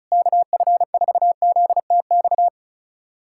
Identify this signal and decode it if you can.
Morse code, heard as a keyed tone, about 36 wpm, 700 Hz. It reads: KF4ZTX
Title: USA Callsign DE Callsign 35wpm